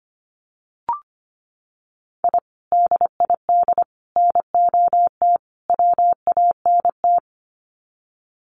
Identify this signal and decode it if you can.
Morse code, heard as a keyed tone, about 25 wpm, 700 Hz. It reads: I DID NOT WANT
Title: Sets of 4 Words 25wpm